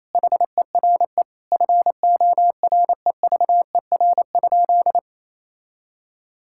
Morse code, heard as a keyed tone, about 28 wpm, 700 Hz. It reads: HERE FOREVER?
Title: Sentences from Top 1600 Words 28wpm